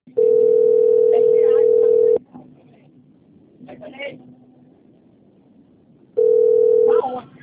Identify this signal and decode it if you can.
Ringback tone (North America), 440 + 480 Hz, with speech.
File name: out- - - - - .71179